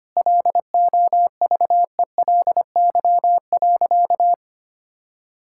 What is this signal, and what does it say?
Morse code, heard as a keyed tone, about 25 wpm, 700 Hz. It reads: LOVELY.
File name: practice / 25wpm / Sentences with 3-Word Phrases - 25wpm